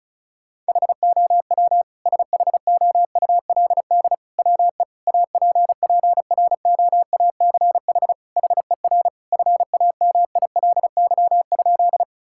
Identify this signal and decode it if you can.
Morse code, heard as a keyed tone, about 34 wpm, 700 Hz. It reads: HOW SHOULD WE APPROACH HER FAMILY?
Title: Sentences from Top 1000 Words - Review Part 6 - 35wpm